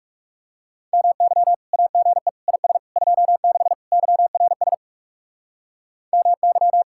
Morse code, heard as a keyed tone, about 38 wpm, 700 Hz. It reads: MY AGE IS 26 YRS MY